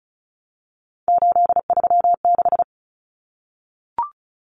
Morse code, heard as a keyed tone, about 34 wpm, 700 Hz. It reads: 836
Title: Numbers - 3-Digit 35wpm